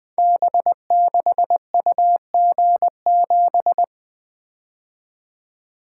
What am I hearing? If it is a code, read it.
Morse code, heard as a keyed tone, about 20 wpm, 700 Hz. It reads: B6UG7
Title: Code Groups - 5 Letter:Numbers - 20wpm